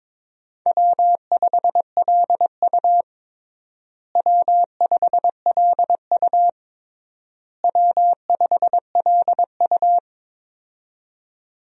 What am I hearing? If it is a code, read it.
Morse code, heard as a keyed tone, about 22 wpm, 700 Hz. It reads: W5LU W5LU W5LU
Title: USA Call Signs - 4 Characters - Repeated 3x - 22wpm